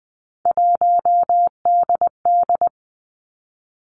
Morse code, heard as a keyed tone, about 20 wpm, 700 Hz. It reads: 1DD
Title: Three Challenging Letter-Number Combinations 20wpm